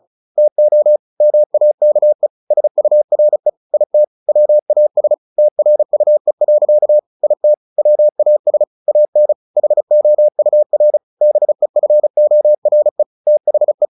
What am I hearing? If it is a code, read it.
Morse code, heard as a keyed tone, about 34 wpm, 600 Hz. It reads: TO MAKE SURE IT WAS TRUE. IT WAS AN HOUR BEFORE THE
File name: the old man and the sea 2 35wpm